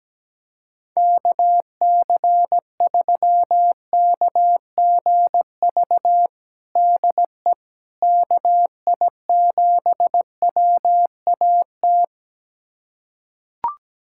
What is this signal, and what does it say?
Morse code, heard as a keyed tone, about 17 wpm, 700 Hz. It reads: KC3KGV DE KI7WAT
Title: USA Callsign DE Callsign 17wpm